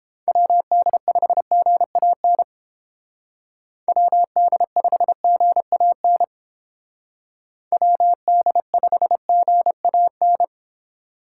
Morse code, heard as a keyed tone, about 30 wpm, 700 Hz. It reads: WD5GAN WD5GAN WD5GAN